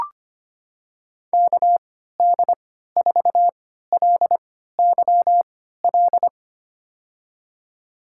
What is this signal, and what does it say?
Morse code, heard as a keyed tone, about 25 wpm, spaced as for 14 wpm, 700 Hz. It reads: KD4LYL
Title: USA Call Signs - 6 Characters - 3x Character Spacing - 25wpm